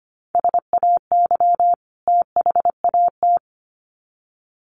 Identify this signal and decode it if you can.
Morse code, heard as a keyed tone, about 25 wpm, 700 Hz. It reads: SAY THAT
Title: Sets of 2 Words 25wpm